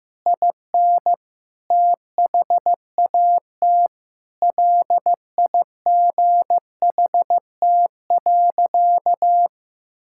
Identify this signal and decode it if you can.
Morse code, heard as a keyed tone, about 14 wpm, 700 Hz. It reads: IN THAT LIGHT.
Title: Sentences from Top 1000 Words - Review Part 6 - 15wpm